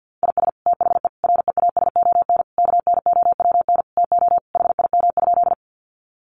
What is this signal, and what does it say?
Morse code, heard as a keyed tone, about 50 wpm, 700 Hz. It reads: IS THE REASON KNOWN TO HIM?
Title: Sentences from Top 700 Words 50wpm